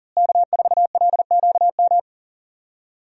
Morse code, heard as a keyed tone, about 37 wpm, 700 Hz. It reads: K4LQM
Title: USA Call Signs - 5 Characters - Repeated 3x - 40wpm